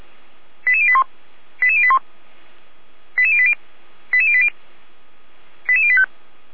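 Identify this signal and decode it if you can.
Five-tone selective call (ZVEI), heard as quick runs of five tones, tones of about 70 ms. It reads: 88081 88081 88080 88080 88085
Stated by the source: ZVEI five-tone selective call 88083, received by SDR.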